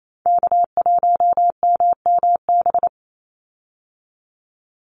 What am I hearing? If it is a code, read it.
Morse code, heard as a keyed tone, about 28 wpm, 700 Hz. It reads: K1MMB
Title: Call Signs - 28wpm